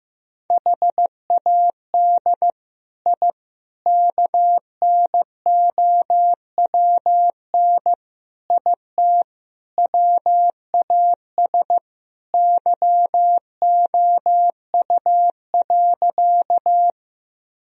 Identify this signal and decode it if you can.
Morse code, heard as a keyed tone, about 15 wpm, 700 Hz. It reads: HAD I KNOWN IT WAS YOU.
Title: Sentences from Top 1000 Words - Review Part 3 - 15wpm